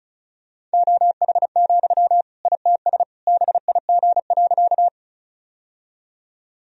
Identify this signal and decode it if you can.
Morse code, heard as a keyed tone, about 34 wpm, 700 Hz. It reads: OH, ITS BIG.
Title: Sentences from Top 1000 Words - Review Part 1 - 35wpm